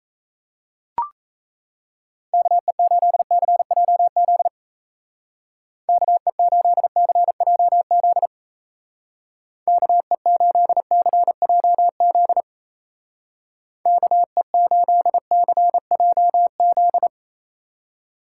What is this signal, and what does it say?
Morse code, heard as a keyed tone, about 34 wpm, 700 Hz. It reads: KE8CJZ KE8CJZ KE8CJZ KE8CJZ
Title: Speed Racing - USA 2x3 Call Signs 28wpm